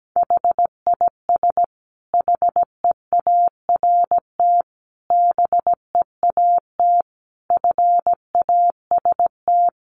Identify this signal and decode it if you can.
Morse code, heard as a keyed tone, about 17 wpm, 700 Hz. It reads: HIS HEART BEAT FAST
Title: Sentences from Top 1300 Words 17wpm